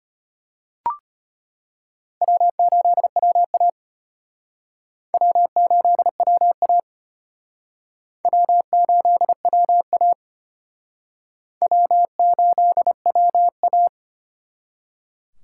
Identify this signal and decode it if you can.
Morse code, heard as a keyed tone, about 34 wpm, 700 Hz. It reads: W8WA W8WA W8WA W8WA